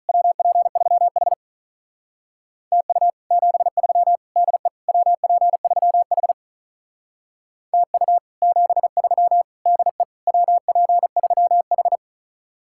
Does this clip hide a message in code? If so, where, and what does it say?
Yes: Morse code from 0.1 to 12.0 s, heard as a keyed tone, about 45 wpm, 700 Hz. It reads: WP3H TU 73 DE WP3H TU 73 DE WP3H